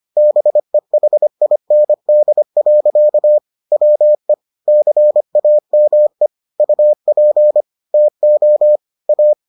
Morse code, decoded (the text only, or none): BEHIND. WE CAME UP TO A